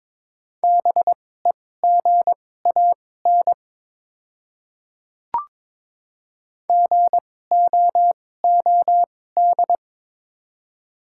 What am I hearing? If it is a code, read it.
Morse code, heard as a keyed tone, about 22 wpm, spaced as for 16 wpm, 700 Hz. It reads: BEGAN GOOD
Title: 2x Character Spacing - Top 300 Words - 22wpm